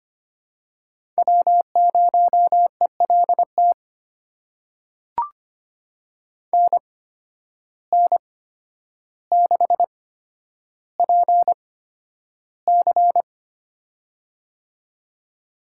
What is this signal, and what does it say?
Morse code, heard as a keyed tone, about 25 wpm, spaced as for 11 wpm, 700 Hz. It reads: W0ELT NN6PC
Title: USA Call Signs - 8x Character Spacing - 25wpm